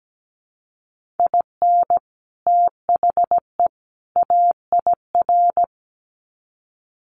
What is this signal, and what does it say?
Morse code, heard as a keyed tone, about 17 wpm, 700 Hz. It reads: IN THE AIR
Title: Sets of 3 Words 17wpm